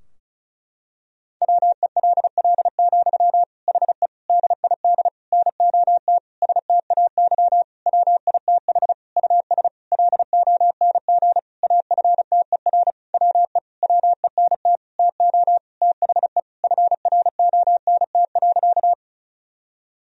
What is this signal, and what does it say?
Morse code, heard as a keyed tone, about 36 wpm, 700 Hz. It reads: WELL, HE DID NOT STAY WITH US LONG AFTER WE WENT TO THE FRONT.
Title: Sentences from Top 1000 Words - Review Part 2 - 35wpm